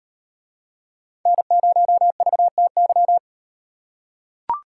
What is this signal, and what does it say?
Morse code, heard as a keyed tone, about 36 wpm, 700 Hz. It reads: N0VTY